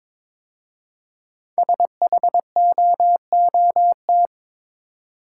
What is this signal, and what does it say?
Morse code, heard as a keyed tone, about 22 wpm, 700 Hz. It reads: SHOOT
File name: 5 Letter Words - First 3 Letters Given - 22wpm